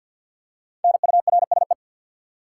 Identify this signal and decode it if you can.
Morse code, heard as a keyed tone, about 49 wpm, 700 Hz. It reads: NURSE
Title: Five Letter Words - 50wpm